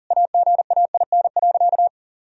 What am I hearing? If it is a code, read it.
Morse code, heard as a keyed tone, about 40 wpm, 700 Hz. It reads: AGAIN.
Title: Sentences from Top 1000 Words - Review Part 1 - 40wpm